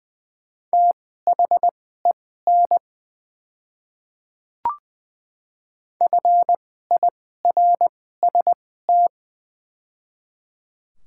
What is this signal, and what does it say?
Morse code, heard as a keyed tone, about 20 wpm, spaced as for 15 wpm, 700 Hz. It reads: THEN FIRST
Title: 2x Character Spacing - Top 100 Words - 20wpm